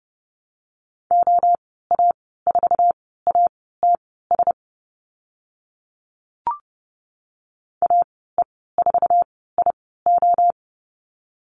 Morse code, heard as a keyed tone, about 30 wpm, spaced as for 17 wpm, 700 Hz. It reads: OA4ATS AE4IO
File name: Lesson 14g - Number 2 - Call Signs 30wpm